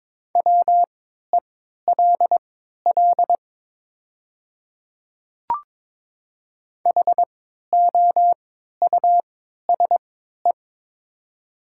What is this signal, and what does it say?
Morse code, heard as a keyed tone, about 22 wpm, spaced as for 12 wpm, 700 Hz. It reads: WELL HOUSE
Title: 3x Character Spacing - Top 300 Words - 22wpm